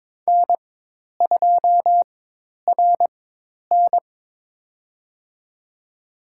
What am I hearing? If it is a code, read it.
Morse code, heard as a keyed tone, about 22 wpm, spaced as for 10 wpm, 700 Hz. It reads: N2RN
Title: USA Call Signs - 4 Characters - 4x Character Spacing - 22wpm